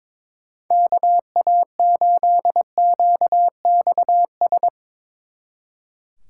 Morse code, heard as a keyed tone, about 23 wpm, 700 Hz. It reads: KA8QXS
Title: Speed Racing - USA 2x3 Call Signs 22wpm